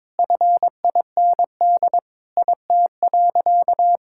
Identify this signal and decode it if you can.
Morse code, heard as a keyed tone, about 22 wpm, 700 Hz. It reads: FIND IT.
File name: practice / 22wpm / Sentences with 3-Word Phrases - 22wpm